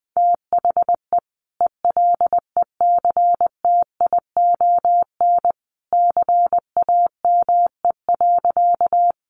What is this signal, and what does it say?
Morse code, heard as a keyed tone, about 20 wpm, 700 Hz. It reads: THE ELECTION CAME.
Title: Sentences from Top 1500 Words 20wpm